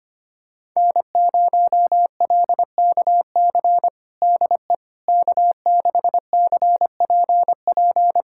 Morse code, heard as a keyed tone, about 25 wpm, 700 Hz. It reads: N0LKC DE K6CPP
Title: USA Callsign DE Callsign - 5 Characters - 25wpm